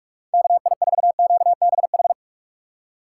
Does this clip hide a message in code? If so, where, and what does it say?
Yes: Morse code from 0.3 to 2.1 s, heard as a keyed tone, about 44 wpm, 700 Hz. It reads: KI4QBH